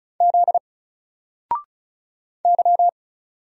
Morse code, heard as a keyed tone, about 35 wpm, 700 Hz. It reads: Z Y
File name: Single-Letter-Number-35wpm